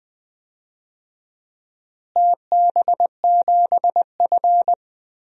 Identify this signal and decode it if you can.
Morse code, heard as a keyed tone, about 20 wpm, 700 Hz. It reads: TB7F
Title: DXCC-100 Call Signs - 4 Characters - Repeated 2x - 20wpm